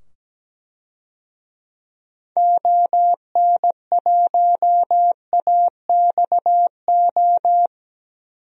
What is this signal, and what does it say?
Morse code, heard as a keyed tone, about 17 wpm, 700 Hz. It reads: ON1AXO
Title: DXCC-100 Call Signs - 6 Characters - 17wpm